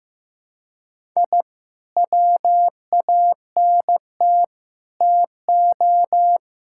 Morse code, heard as a keyed tone, about 15 wpm, 700 Hz. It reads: I WANT TO
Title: Sentences from Top 1000 Words - Review Part 6 - 15wpm